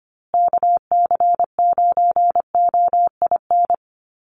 Morse code, heard as a keyed tone, about 25 wpm, 700 Hz. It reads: KC9OIN
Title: USA Call Signs Repeated 2x 25wpm